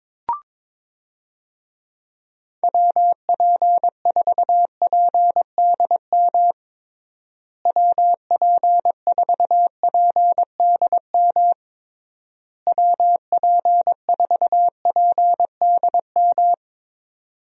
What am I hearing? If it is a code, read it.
Morse code, heard as a keyed tone, about 22 wpm, 700 Hz. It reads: WP4PDM WP4PDM WP4PDM
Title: USA Call Signs - 6 Characters - Repeated 3x - 22wpm